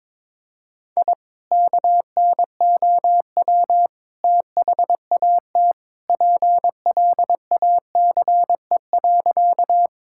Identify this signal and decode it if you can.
Morse code, heard as a keyed tone, about 22 wpm, 700 Hz. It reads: I KNOW THAT PLACE.
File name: practice / 22wpm / Sentences with 2-Word Phrases - 22wpm